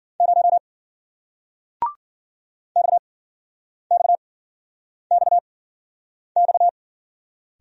Morse code, heard as a keyed tone, about 56 wpm, spaced as for 9 wpm, 700 Hz. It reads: 0 XXXX